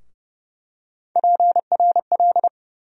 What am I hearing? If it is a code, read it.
Morse code, heard as a keyed tone, about 31 wpm, 700 Hz. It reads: PRL
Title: USA Call Sign Postfixes - 3 Letters 30wpm